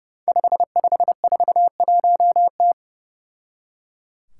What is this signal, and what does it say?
Morse code, heard as a keyed tone, about 30 wpm, 700 Hz. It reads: 5541T
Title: Five Challenging Letter-Number Combinations 30wpm